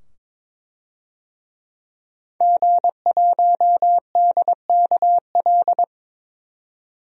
Morse code, heard as a keyed tone, about 22 wpm, 700 Hz. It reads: G1DKL
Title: DXCC-100 Call Signs - 5 Characters - 22wpm